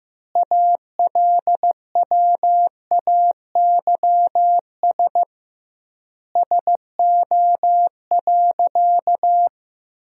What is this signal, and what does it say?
Morse code, heard as a keyed tone, about 15 wpm, 700 Hz. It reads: ALWAYS SO.
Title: 2x Word-Spacing - Sentences from Top 300 Words - 15wpm